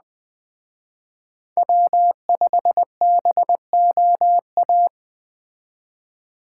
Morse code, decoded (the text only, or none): W5BOA